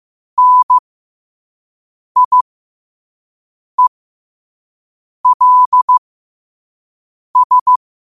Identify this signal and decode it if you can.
Morse code, heard as a keyed tone, about 15 wpm, spaced as for 5 wpm, 1000 Hz. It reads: NIELS
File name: 5wpm